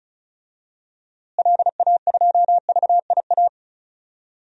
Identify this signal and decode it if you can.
Morse code, heard as a keyed tone, about 34 wpm, 700 Hz. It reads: LA2VIA
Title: DXCC-100 Call Signs - 6 Characters - Repeated 3x - 35wpm